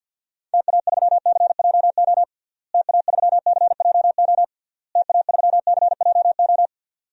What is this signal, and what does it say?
Morse code, heard as a keyed tone, about 46 wpm, 700 Hz. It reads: TA3CJO TA3CJO TA3CJO
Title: DXCC-100 Call Signs - 6 Characters - Repeated 3x - 50wpm